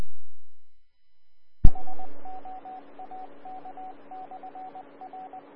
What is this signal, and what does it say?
Morse code, heard as a keyed tone, about 20 wpm, 750 Hz, with radio noise. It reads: SOAK/R
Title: Austin, TX Amateur Repeaters